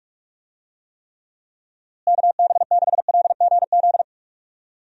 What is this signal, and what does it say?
Morse code, heard as a keyed tone, about 46 wpm, 700 Hz. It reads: KB6LGZ
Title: USA Call Signs - 6 Characters - Repeated 2x - 45wpm